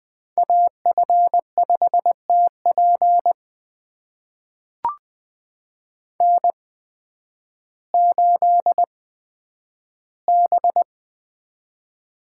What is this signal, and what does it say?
Morse code, heard as a keyed tone, about 20 wpm, spaced as for 11 wpm, 700 Hz. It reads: AF5TP N8B
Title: USA Call Signs - 5 Characters - 8x Character Spacing - 20wpm